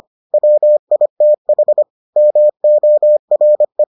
Morse code, heard as a keyed tone, about 25 wpm, 600 Hz. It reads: WITH MORE